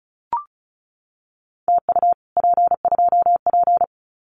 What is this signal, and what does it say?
Morse code, heard as a keyed tone, about 34 wpm, 700 Hz. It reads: TU P2P
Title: Common QSO Phrases Encore 35wpm